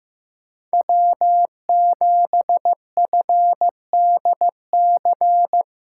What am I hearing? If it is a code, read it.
Morse code, heard as a keyed tone, about 15 wpm, 700 Hz. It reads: W7FDC
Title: USA Callsign DE Callsign - 5 Characters - 15wpm